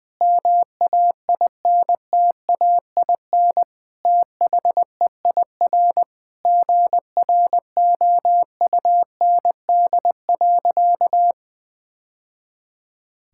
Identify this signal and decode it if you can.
Morse code, heard as a keyed tone, about 20 wpm, 700 Hz. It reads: MAINTAIN THEIR GROUND.
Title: Sentences from Top 1000 Words - Review Part 5 - 20wpm